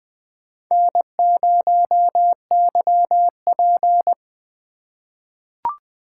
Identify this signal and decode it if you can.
Morse code, heard as a keyed tone, about 20 wpm, 700 Hz. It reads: N0YP
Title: USA Call Signs - 4 Characters - 6x Character Spacing - 20wpm